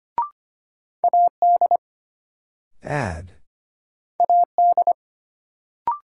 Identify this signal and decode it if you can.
Morse code, heard as a keyed tone, about 25 wpm, 700 Hz. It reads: AD AD
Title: Sentences from Top 1300 Words - New Word Drill - 25wpm